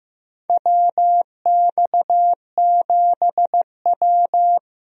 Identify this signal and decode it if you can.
Morse code, heard as a keyed tone, about 15 wpm, 700 Hz. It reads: WX7W
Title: USA Call Signs - 4 Characters - 15wpm